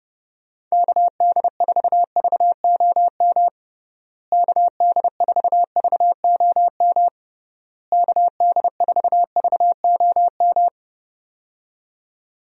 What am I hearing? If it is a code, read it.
Morse code, heard as a keyed tone, about 30 wpm, 700 Hz. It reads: KD4VOM KD4VOM KD4VOM
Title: USA Call Signs - 6 Characters - Repeated 3x - 30wpm